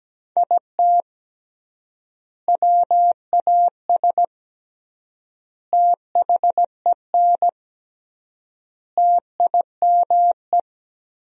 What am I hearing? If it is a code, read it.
Morse code, heard as a keyed tone, about 17 wpm, 700 Hz. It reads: IT WAS THEN TIME
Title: 3x Word-Spacing - Sentences from Top 300 Words - 17wpm